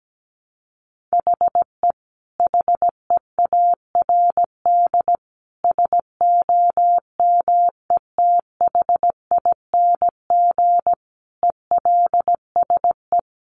Morse code, heard as a keyed tone, about 17 wpm, 700 Hz. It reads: HE HEARD SOMETHING ELSE
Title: Sentences from Top 800 Words 17wpm